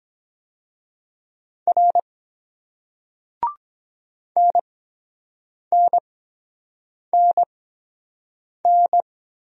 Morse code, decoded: R NNNN

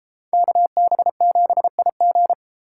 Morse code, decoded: KB7IG

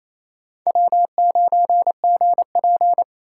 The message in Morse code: W9GP

